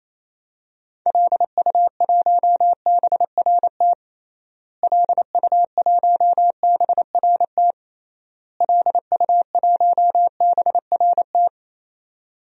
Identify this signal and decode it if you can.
Morse code, heard as a keyed tone, about 27 wpm, 700 Hz. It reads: LU1BRT LU1BRT LU1BRT
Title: DXCC-100 Call Signs - 6 Characters - Repeated 3x - 28wpm